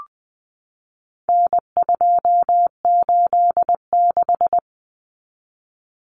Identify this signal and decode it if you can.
Morse code, heard as a keyed tone, about 20 wpm, 700 Hz. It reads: N286
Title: Common Cut Numbers - 4-Digit 20wpm